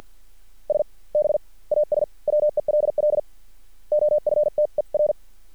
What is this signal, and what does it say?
Morse code, heard as a keyed tone, about 50 wpm, spaced as for 40 wpm, 600 Hz. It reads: SB AS WELL OFTER
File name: archived / CRSnet